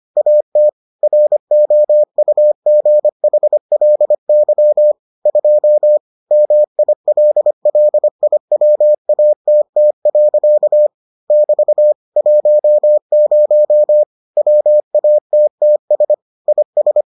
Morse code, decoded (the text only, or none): AT ROUGHLY 2 MILLIWATT. <BT> 10 WATTS IS